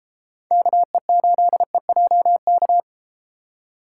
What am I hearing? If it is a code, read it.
Morse code, heard as a keyed tone, about 31 wpm, 700 Hz. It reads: KE8EJK